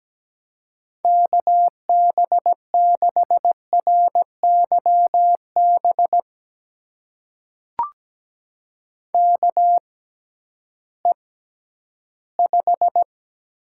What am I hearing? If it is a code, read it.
Morse code, heard as a keyed tone, about 17 wpm, spaced as for 11 wpm, 700 Hz. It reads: KB6RYB KE5
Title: USA Call Signs - 6 Characters - 6x Character Spacing - 17wpm